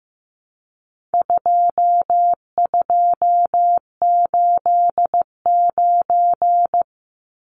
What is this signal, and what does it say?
Morse code, heard as a keyed tone, about 15 wpm, 700 Hz. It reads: 2289
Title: Numbers - 4-Digit 15wpm